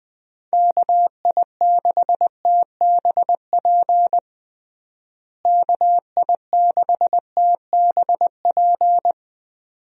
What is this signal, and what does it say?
Morse code, heard as a keyed tone, about 21 wpm, 700 Hz. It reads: KI6TBP KI6TBP
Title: USA Call Signs - 6 Characters - Repeated 2x - 20wpm